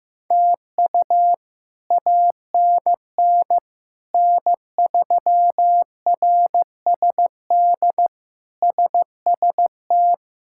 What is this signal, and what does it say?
Morse code, heard as a keyed tone, about 15 wpm, 700 Hz. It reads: TU ANN N3RSD SST
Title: K1USN SST QSO Practice - 15wpm